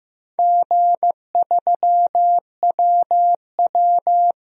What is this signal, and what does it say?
Morse code, heard as a keyed tone, about 15 wpm, 700 Hz. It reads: G3WW
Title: DXCC-100 Call Signs - 4 Characters - Repeated 3x - 15wpm